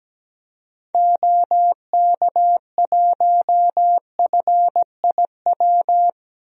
Morse code, heard as a keyed tone, about 17 wpm, 700 Hz. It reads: OK1FIW
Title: DXCC-100 Call Signs - 6 Characters - 17wpm